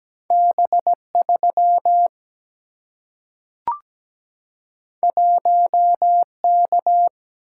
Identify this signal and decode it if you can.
Morse code, heard as a keyed tone, about 17 wpm, 700 Hz. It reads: B3 1K
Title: Single-Letter Single-Number 17wpm